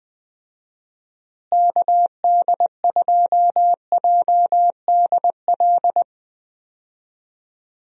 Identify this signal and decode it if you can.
Morse code, heard as a keyed tone, about 21 wpm, 700 Hz. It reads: KD2JDL
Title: USA Call Signs - 6 Characters - Repeated 3x - 20wpm